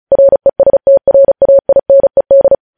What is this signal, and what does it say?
Morse code, heard as a keyed tone, about 36 wpm, 550 Hz. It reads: RESTRAINED
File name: podcast / News headlines-1x-35x35